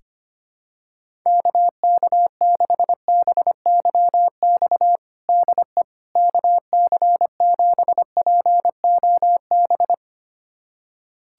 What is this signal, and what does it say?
Morse code, heard as a keyed tone, about 25 wpm, 700 Hz. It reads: KK6BYX DE KC7POB
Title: USA Callsign DE Callsign - 6 Characters - 25wpm